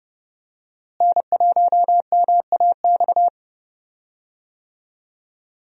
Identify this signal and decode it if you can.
Morse code, heard as a keyed tone, about 30 wpm, 700 Hz. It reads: N1MAX
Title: USA Call Signs - 5 Characters - Repeated 3x - 30wpm